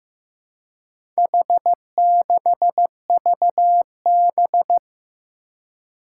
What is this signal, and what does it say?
Morse code, heard as a keyed tone, about 15 wpm, 700 Hz. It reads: H6VB
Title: Four Challenging Letter-Number Combinations 15wpm